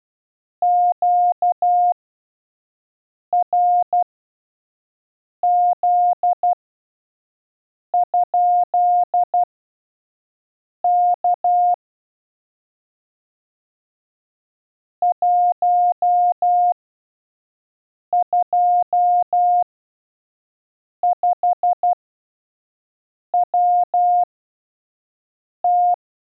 Morse code, heard as a keyed tone, about 12 wpm, spaced as for 5 wpm, 710 Hz. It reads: QRZ? K 125WT
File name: 5WPM test 1